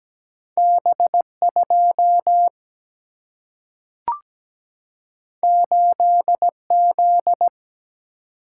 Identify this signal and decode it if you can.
Morse code, heard as a keyed tone, about 17 wpm, 700 Hz. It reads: B 2 8 Z
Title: Single-Letter Single-Number 17wpm